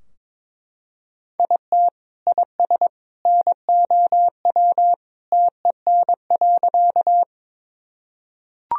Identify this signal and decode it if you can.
Morse code, heard as a keyed tone, about 21 wpm, 700 Hz. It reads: IT IS NOW TEN.
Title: Sentences from Top 1000 Words - Review Part 2 - 22wpm